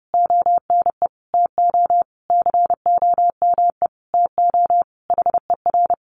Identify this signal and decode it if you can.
Morse code, heard as a keyed tone, about 30 wpm, 700 Hz. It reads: ONE TO COME TO HER
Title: Sentences from Top 1300 Words 30wpm